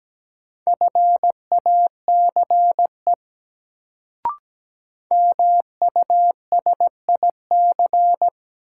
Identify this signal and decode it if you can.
Morse code, heard as a keyed tone, about 17 wpm, 700 Hz. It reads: FACE MUSIC
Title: Sentences from Top 300 Words - New Word Drill - 17wpm